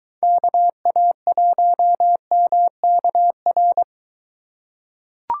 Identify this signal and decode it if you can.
Morse code, heard as a keyed tone, about 22 wpm, 700 Hz. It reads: KA1MKR